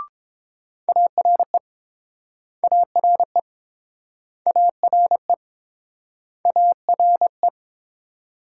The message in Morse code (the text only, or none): ARE ARE ARE ARE